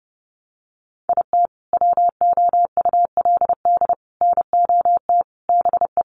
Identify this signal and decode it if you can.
Morse code, heard as a keyed tone, about 28 wpm, 700 Hz. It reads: IT WOULD NOT BE
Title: Sentences from Top 600 Words 30wpm